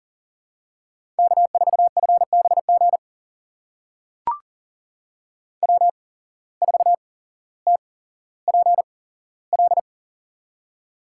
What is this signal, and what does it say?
Morse code, heard as a keyed tone, about 40 wpm, spaced as for 17 wpm, 700 Hz. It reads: K4FBG W4TPL
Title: USA Call Signs - 8x Character Spacing - 40wpm